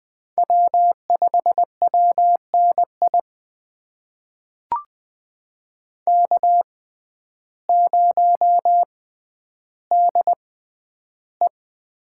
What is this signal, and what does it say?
Morse code, heard as a keyed tone, about 20 wpm, spaced as for 11 wpm, 700 Hz. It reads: W5WNI K0DE
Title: USA Call Signs - 5 Characters - 6x Character Spacing - 20wpm